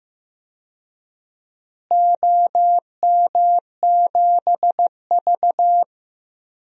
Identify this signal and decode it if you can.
Morse code, heard as a keyed tone, about 15 wpm, 700 Hz. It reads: OM7V
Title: DXCC-100 Call Signs - 4 Characters - Repeated 3x - 15wpm